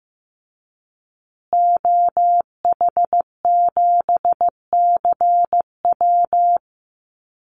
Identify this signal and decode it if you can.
Morse code, heard as a keyed tone, about 15 wpm, 700 Hz. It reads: OH7CW
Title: Call Signs Repeated 3x - 15wpm